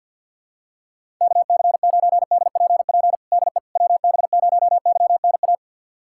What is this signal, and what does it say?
Morse code, heard as a keyed tone, about 47 wpm, 690 Hz. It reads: KC9DPP DE WB0YNA
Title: USA Callsign DE Callsign - 6 Characters - 50wpm